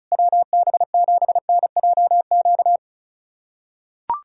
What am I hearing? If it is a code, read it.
Morse code, heard as a keyed tone, about 34 wpm, 690 Hz. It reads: WB7NJQ
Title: USA Callsign DE Callsign - 6 Characters - 35wpm